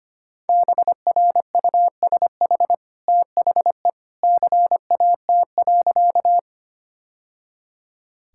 Morse code, heard as a keyed tone, about 25 wpm, 700 Hz. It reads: BRUSH THE CAT.
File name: Easy Sentences with Hints - 25wpm